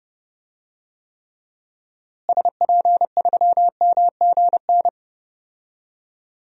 Morse code, heard as a keyed tone, about 32 wpm, 700 Hz. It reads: SP3MGN
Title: DXCC-100 Call Signs - 6 Characters - Repeated 3x - 30wpm